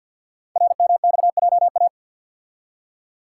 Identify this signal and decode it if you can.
Morse code, heard as a keyed tone, about 50 wpm, 700 Hz. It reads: RMXJA
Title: Code Groups - 5 Letters - 50wpm